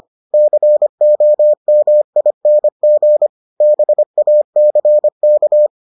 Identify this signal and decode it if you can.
Morse code, heard as a keyed tone, about 25 wpm, 600 Hz. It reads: COMING BACK